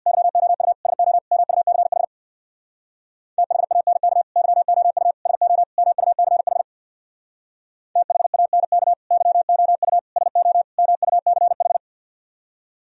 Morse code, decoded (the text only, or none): YOU SO MUCH THANK YOU SO MUCH THANK YOU SO MUCH